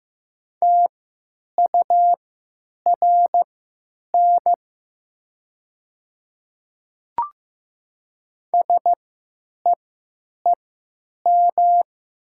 Morse code, heard as a keyed tone, about 15 wpm, spaced as for 9 wpm, 700 Hz. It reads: TURN SEEM
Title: 3x Character Spacing - Top 300 Words - 15wpm